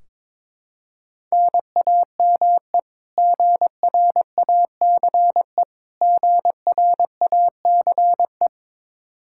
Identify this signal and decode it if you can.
Morse code, heard as a keyed tone, about 22 wpm, 700 Hz. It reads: NAME GRACE GRACE
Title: Practice QSO - Exchange 1 - Beginning - 22wpm